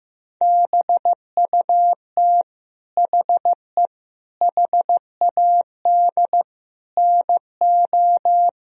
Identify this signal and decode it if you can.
Morse code, heard as a keyed tone, about 15 wpm, 700 Hz. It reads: BUT HE HAD NO
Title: Sentences from Top 1000 Words - Review Part 4 - 15wpm